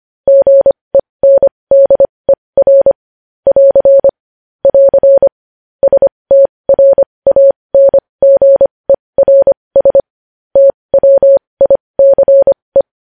Morse code, heard as a keyed tone, about 25 wpm, spaced as for 20 wpm, 550 Hz. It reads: GENDER <AR> <AR> STRANGERS TWICE
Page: news-headlines-in-morse-code-at-20-wpm